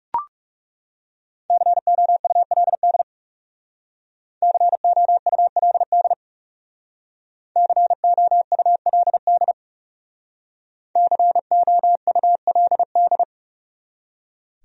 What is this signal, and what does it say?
Morse code, heard as a keyed tone, about 40 wpm, 700 Hz. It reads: COULD COULD COULD COULD